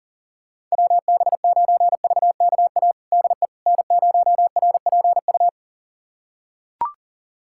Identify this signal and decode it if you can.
Morse code, heard as a keyed tone, about 40 wpm, 700 Hz. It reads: WB9VKA DE N0RPU
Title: USA Callsign DE Callsign 40wpm